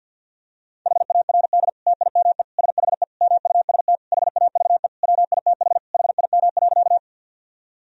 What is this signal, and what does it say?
Morse code, heard as a keyed tone, about 47 wpm, 700 Hz. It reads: HARD TIME SHE MUST HAVE WITH HIM.
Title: Sentences from Top 1000 Words - Review Part 1 - 50wpm